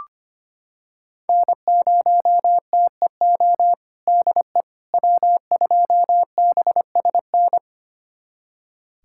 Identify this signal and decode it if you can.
Morse code, heard as a keyed tone, about 25 wpm, 700 Hz. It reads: N0TEO DE W2BSN
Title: USA Callsign DE Callsign - 5 Characters - 25wpm